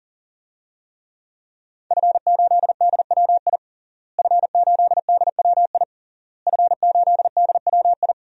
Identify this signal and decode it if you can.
Morse code, heard as a keyed tone, about 40 wpm, 700 Hz. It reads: F8DWI F8DWI F8DWI
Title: DXCC-100 Call Signs - 5 Characters - Repeated 3x - 40wpm